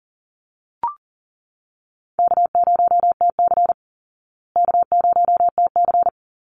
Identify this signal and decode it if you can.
Morse code, heard as a keyed tone, about 40 wpm, 700 Hz. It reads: K0TC K0TC
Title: Call Signs Repeated 3x - 40wpm